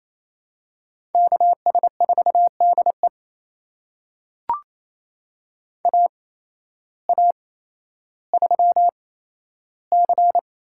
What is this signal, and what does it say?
Morse code, heard as a keyed tone, about 27 wpm, spaced as for 13 wpm, 700 Hz. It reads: KS4DE AA3C
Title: USA Call Signs - 8x Character Spacing - 28wpm